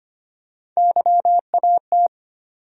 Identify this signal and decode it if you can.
Morse code, heard as a keyed tone, about 25 wpm, 700 Hz. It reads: YAT
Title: USA Call Sign Postfixes - 3 Letters 25wpm